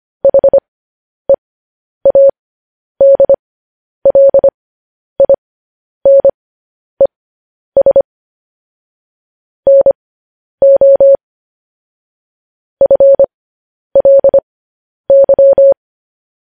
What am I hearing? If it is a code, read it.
Morse code, heard as a keyed tone, about 25 wpm, spaced as for 10 wpm, 550 Hz. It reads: HEADLINES NO FLY
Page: news headlines in morse code at 10 wpm